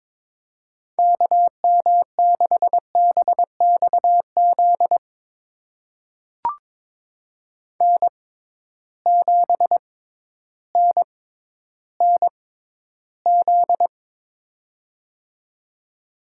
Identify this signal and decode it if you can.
Morse code, heard as a keyed tone, about 22 wpm, spaced as for 12 wpm, 700 Hz. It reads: KM6BXZ N7NNZ
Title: USA Call Signs - 6x Character Spacing - 22wpm